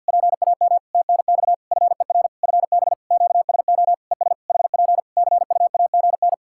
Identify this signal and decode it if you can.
Morse code, heard as a keyed tone, about 50 wpm, 700 Hz. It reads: PAM TNX FER FB QSO ES HP CUAGN
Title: K1USN SST QSO Practice 50wpm